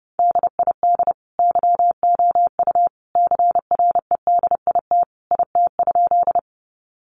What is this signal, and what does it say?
Morse code, heard as a keyed tone, about 30 wpm, 700 Hz. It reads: DID YOU CREDIT IT?
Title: Sentences from Top 1000 Words 30wpm